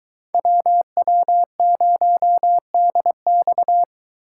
Morse code, heard as a keyed tone, about 21 wpm, 700 Hz. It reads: WW0DX